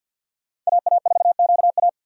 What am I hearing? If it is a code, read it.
Morse code, heard as a keyed tone, about 51 wpm, 690 Hz. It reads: AA4QA
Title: USA Call Signs - 2x Character Spacing - 50wpm